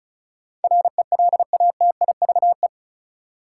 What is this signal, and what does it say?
Morse code, heard as a keyed tone, about 36 wpm, 700 Hz. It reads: RELATIVE
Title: Sentences from Top 1400 Words - New Word Drill - 35wpm